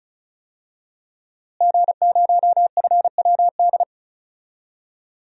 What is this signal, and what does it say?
Morse code, heard as a keyed tone, about 34 wpm, 700 Hz. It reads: G0FWD
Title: DXCC-100 Call Signs - 5 Characters - Repeated 2x - 35wpm